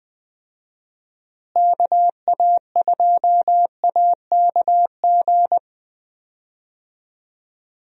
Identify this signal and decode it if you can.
Morse code, heard as a keyed tone, about 21 wpm, 700 Hz. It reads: KA2AKG
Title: USA Call Signs - 6 Characters - Repeated 3x - 20wpm